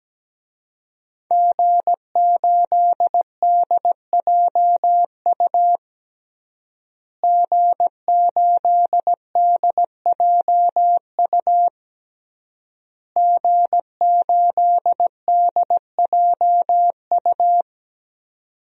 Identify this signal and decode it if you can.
Morse code, heard as a keyed tone, about 17 wpm, 700 Hz. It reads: G8DJU G8DJU G8DJU
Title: DXCC-100 Call Signs - 5 Characters - Repeated 3x - 17wpm